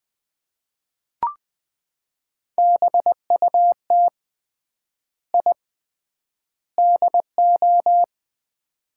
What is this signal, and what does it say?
Morse code, heard as a keyed tone, about 21 wpm, 700 Hz. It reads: BUT I DO
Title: 3x Word-Spacing - Sets of 3 Words - 20wpm